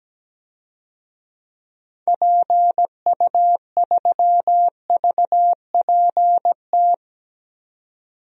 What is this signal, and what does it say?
Morse code, heard as a keyed tone, about 17 wpm, 700 Hz. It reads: PU3VPT
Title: DXCC-100 Call Signs - 6 Characters - 17wpm